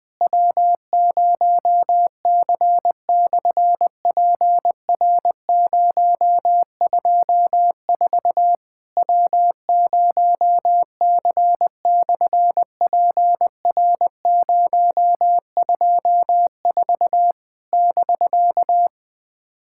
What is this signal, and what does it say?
Morse code, heard as a keyed tone, about 20 wpm, 700 Hz. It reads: W0C/PR024 W0C/PR024 <BK>
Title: SOTA QSO Practice 20wpm